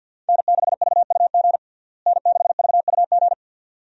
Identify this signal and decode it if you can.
Morse code, heard as a keyed tone, about 53 wpm, 700 Hz. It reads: N6VUG N6VUG
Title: USA Call Signs - 5 Characters - Repeated 2x - 50wpm